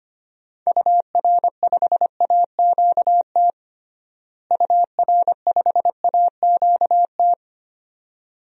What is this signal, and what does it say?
Morse code, heard as a keyed tone, about 25 wpm, 700 Hz. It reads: UR5AQT UR5AQT
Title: DXCC-100 Call Signs - 6 Characters - Repeated 2x - 25wpm